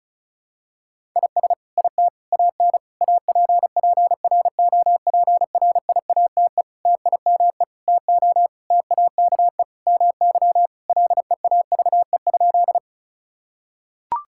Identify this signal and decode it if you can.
Morse code, heard as a keyed tone, about 34 wpm, 700 Hz. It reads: IS IT AN APPROPRIATE TIME TO TAKE MY LEAVE?
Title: Sentences from Top 1000 Words - Review Part 5 - 35wpm